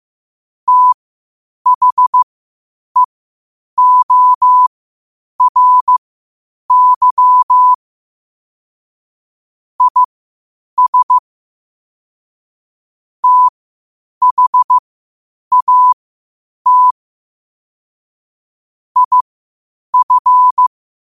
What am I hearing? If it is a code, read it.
Morse code, heard as a keyed tone, about 15 wpm, spaced as for 8 wpm, 1000 Hz. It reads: THEORY IS THAT IF
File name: qotdfiles / 8wpm